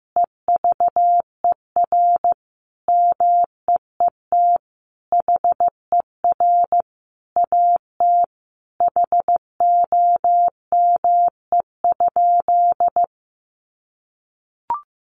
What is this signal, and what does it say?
Morse code, heard as a keyed tone, about 15 wpm, 700 Hz. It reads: EVER MEET HER AT HOME?
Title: Sentences from Top 1100 Words 15wpm